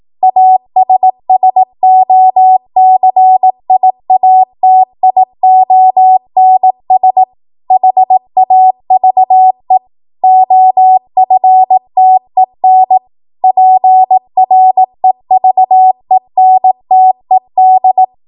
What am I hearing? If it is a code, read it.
Morse code, heard as a keyed tone, about 18 wpm, 750 Hz. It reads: ASSOCIATIONS HAVE OFTEN PREVENTED
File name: Morse / 18WPM